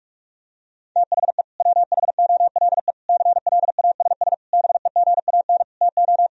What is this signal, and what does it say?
Morse code, heard as a keyed tone, about 46 wpm, 700 Hz. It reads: THE WHOLE CLASS BEGAN TO
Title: Sentences from Top 1000 Words - Review Part 2 - 45wpm